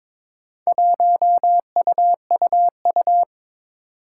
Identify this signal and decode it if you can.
Morse code, heard as a keyed tone, about 22 wpm, 700 Hz. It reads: 1UUU
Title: Four Challenging Letter-Number Combinations 22wpm